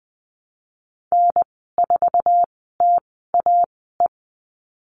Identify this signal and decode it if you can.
Morse code, heard as a keyed tone, about 20 wpm, spaced as for 14 wpm, 700 Hz. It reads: N4TAE
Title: 055 Lesson 10g - Letter r - Call Signs 20wpm